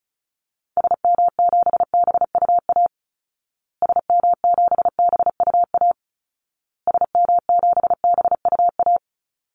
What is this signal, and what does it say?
Morse code, heard as a keyed tone, about 34 wpm, 700 Hz. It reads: SM7BUA SM7BUA SM7BUA
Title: Call Signs Repeated 2x - 35wpm